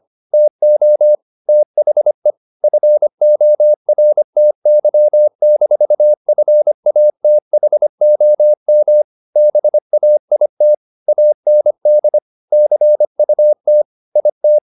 Morse code, heard as a keyed tone, about 25 wpm, 600 Hz. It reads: TO THE FORTY-FATHOM BAIT AND CUT IT
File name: the old man and the sea part 1 25wpm